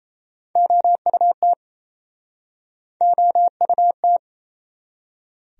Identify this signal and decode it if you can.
Morse code, heard as a keyed tone, about 40 wpm, 700 Hz. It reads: OUT OUT